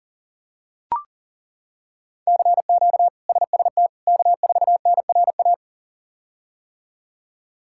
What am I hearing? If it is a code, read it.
Morse code, heard as a keyed tone, about 40 wpm, 700 Hz. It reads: CQ SST K4NRA
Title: K1USN SST QSO Practice - 40wpm